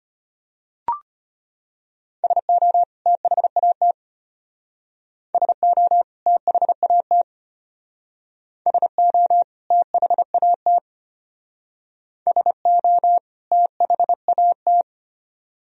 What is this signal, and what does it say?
Morse code, heard as a keyed tone, about 32 wpm, 700 Hz. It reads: SO THAT SO THAT SO THAT SO THAT